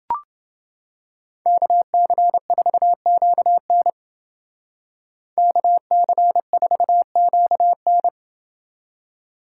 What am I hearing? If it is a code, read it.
Morse code, heard as a keyed tone, about 29 wpm, 700 Hz. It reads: KC4QN KC4QN